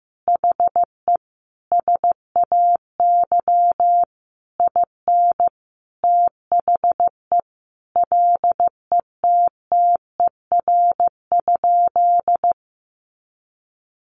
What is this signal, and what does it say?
Morse code, heard as a keyed tone, about 15 wpm, 700 Hz. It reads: HE SAY IN THE LETTER?
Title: Sentences from Top 300 Words 15wpm